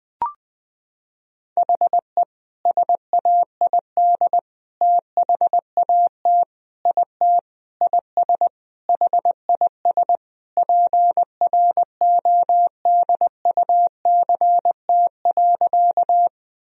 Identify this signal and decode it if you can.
Morse code, heard as a keyed tone, about 20 wpm, 700 Hz. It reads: HE SAID THAT IT IS HIS PRODUCT.
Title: Sentences from Top 1000 Words - Review Part 2 - 20wpm